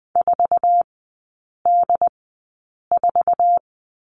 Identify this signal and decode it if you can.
Morse code, heard as a keyed tone, about 20 wpm, spaced as for 8 wpm, 700 Hz. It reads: 4D4
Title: 081 Lesson 14e - Number 2 - Three Characters 20wpm